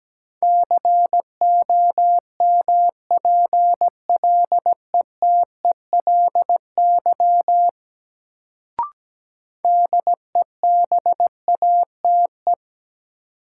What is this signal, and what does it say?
Morse code, heard as a keyed tone, about 17 wpm, 700 Hz. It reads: COMPLETELY DEBATE
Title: Sentences from Top 1100 Words - New Word Drill Alt - 17wpm